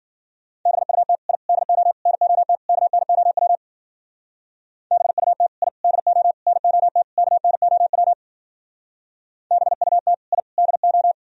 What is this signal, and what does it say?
Morse code, heard as a keyed tone, about 50 wpm, 700 Hz. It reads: BUT I DO NOT KNOW BUT I DO NOT KNOW BUT I DO
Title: Speed-Racing - Sets of 5 Words 40wpm